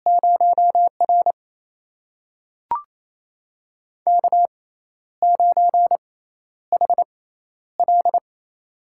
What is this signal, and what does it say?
Morse code, heard as a keyed tone, about 28 wpm, spaced as for 12 wpm, 700 Hz. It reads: 0R K9HL
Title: USA Call Signs - 4 Characters - 6x Character Spacing - 28wpm